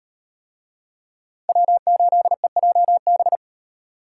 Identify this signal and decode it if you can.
Morse code, heard as a keyed tone, about 34 wpm, 700 Hz. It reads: W8EJB